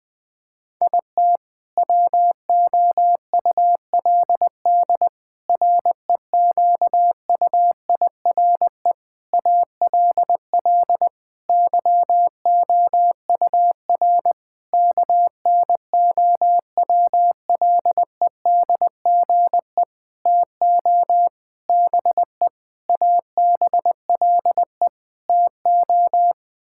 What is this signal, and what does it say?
Morse code, heard as a keyed tone, about 20 wpm, 700 Hz. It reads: IT WOULD REQUIRE ALL YOUR KNOWLEDGE TO BE ABLE TO
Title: Sentences from Top 1000 Words - Review Part 6 - 20wpm